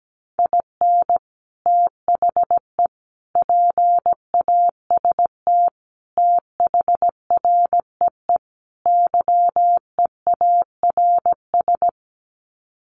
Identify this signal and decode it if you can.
Morse code, heard as a keyed tone, about 17 wpm, 700 Hz. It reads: IN THE PAST THREE YEARS
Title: Sets of 5 Words Encore2 17wpm